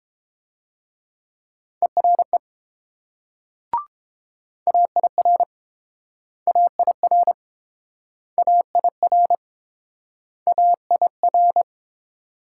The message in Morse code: ERE AIR AIR AIR AIR

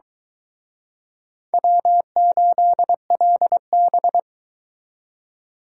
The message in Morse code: W8LB

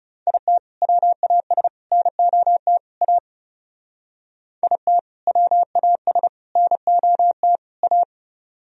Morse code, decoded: IT WAS NOT A IT WAS NOT A

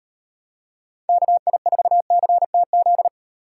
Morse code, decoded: KI4CTZ

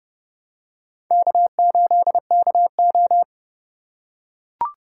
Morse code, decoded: K8KO

